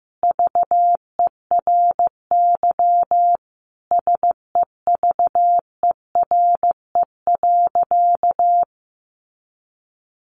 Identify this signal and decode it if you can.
Morse code, heard as a keyed tone, about 14 wpm, 700 Hz. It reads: VERY SEVERE.
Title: Sentences from Top 1400 Words 15wpm